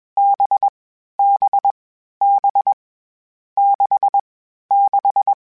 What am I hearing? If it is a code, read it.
Morse code, heard as a keyed tone, about 21 wpm, 800 Hz. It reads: B B B 6 6
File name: Short Runs / Short Runs-mp3s / NDB NDB6 @20